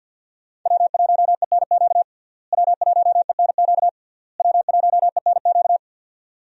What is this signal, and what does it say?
Morse code, heard as a keyed tone, about 46 wpm, 700 Hz. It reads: W1ENQ W1ENQ W1ENQ
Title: USA Call Signs - 5 Characters - Repeated 3x - 50wpm